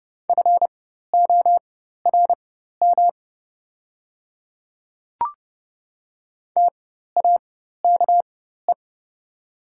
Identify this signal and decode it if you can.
Morse code, heard as a keyed tone, about 30 wpm, spaced as for 14 wpm, 700 Hz. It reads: FORM TAKE
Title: 4x Character Spacing - Top 300 Words - 30wpm